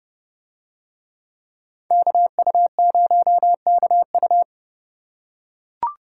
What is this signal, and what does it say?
Morse code, heard as a keyed tone, about 28 wpm, 700 Hz. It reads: KU0KU